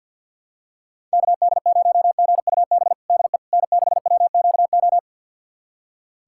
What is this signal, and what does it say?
Morse code, heard as a keyed tone, about 50 wpm, 700 Hz. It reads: KD0GUB DE N6WQO
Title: USA Callsign DE Callsign 50wpm